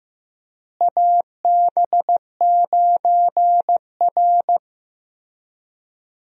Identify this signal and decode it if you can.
Morse code, heard as a keyed tone, about 15 wpm, 700 Hz. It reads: AB9R
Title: USA Call Signs - 4 Characters - Repeated 3x - 15wpm